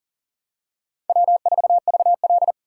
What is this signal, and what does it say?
Morse code, heard as a keyed tone, about 40 wpm, 700 Hz. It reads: W4VL
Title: USA Call Signs - 4 Characters - 4x Character Spacing - 40wpm